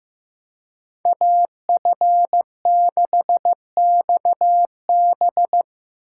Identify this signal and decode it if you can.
Morse code, heard as a keyed tone, about 15 wpm, 700 Hz. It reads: AF6XB
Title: USA Call Signs - 6x Character Spacing - 15wpm